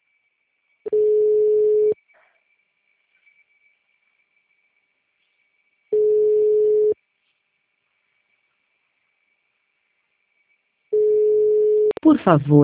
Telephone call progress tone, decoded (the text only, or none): ringback tone